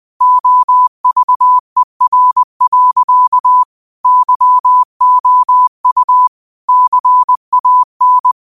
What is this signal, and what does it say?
Morse code, heard as a keyed tone, about 20 wpm, 1000 Hz. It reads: OVER. YOU CAN
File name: qotdfiles / 20wpm